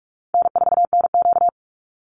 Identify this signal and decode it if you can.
Morse code, heard as a keyed tone, about 44 wpm, 700 Hz. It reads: N4NQ
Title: Easy Call Signs Repeated 3x - 45wpm